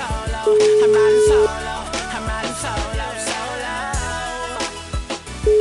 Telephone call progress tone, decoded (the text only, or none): ringback tone